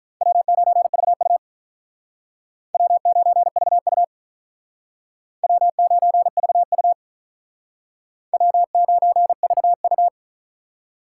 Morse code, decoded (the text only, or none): W9VU W9VU W9VU W9VU